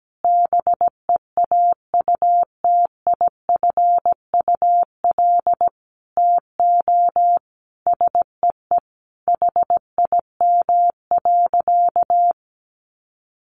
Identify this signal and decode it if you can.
Morse code, heard as a keyed tone, about 17 wpm, 700 Hz. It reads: BEAUTIFUL TO SEE HIM.
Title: Sentences from Top 900 Words 17wpm